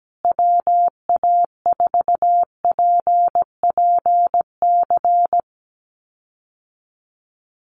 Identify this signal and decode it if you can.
Morse code, heard as a keyed tone, about 17 wpm, 700 Hz. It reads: WA4PPC
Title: USA Call Signs Repeated 2x 17wpm